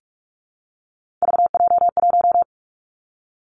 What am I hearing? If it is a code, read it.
Morse code, heard as a keyed tone, about 46 wpm, 700 Hz. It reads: VJ1
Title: Three Challenging Letter-Number Combinations 45wpm